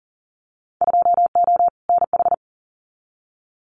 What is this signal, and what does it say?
Morse code, heard as a keyed tone, about 40 wpm, 700 Hz. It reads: 2O NH
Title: Winter Field Day Exchanges 40wpm